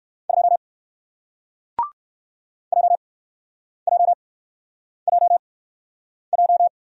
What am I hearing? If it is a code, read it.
Morse code, heard as a keyed tone, about 61 wpm, 700 Hz. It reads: 2 J J J J